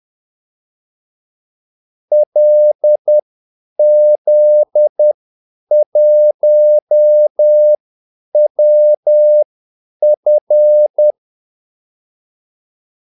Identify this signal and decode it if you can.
Morse code, heard as a keyed tone, about 10 wpm, spaced as for 8 wpm, 600 Hz. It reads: LZ1WF
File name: Callsigns-10-8wpm